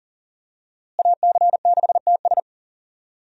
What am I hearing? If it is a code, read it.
Morse code, heard as a keyed tone, about 40 wpm, 700 Hz. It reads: AC6TS
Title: USA Call Signs - 5 Characters - Repeated 2x - 40wpm